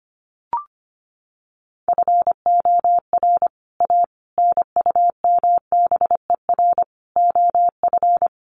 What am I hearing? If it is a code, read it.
Morse code, heard as a keyed tone, about 25 wpm, 700 Hz. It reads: FOR A NUMBER OF
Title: Sets of 5 Words Encore 25wpm